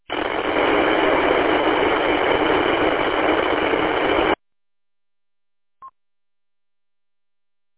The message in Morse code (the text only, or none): none